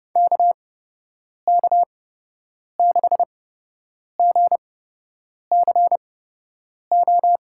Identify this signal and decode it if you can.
Morse code, heard as a keyed tone, about 30 wpm, spaced as for 8 wpm, 700 Hz. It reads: KK6GCO
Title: USA Call Signs - 8x Character Spacing - 30wpm